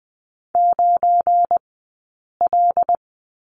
Morse code, heard as a keyed tone, about 20 wpm, spaced as for 8 wpm, 700 Hz. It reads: 9L
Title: 217 Lesson 35d - Number 0 - Two Characters 20wpm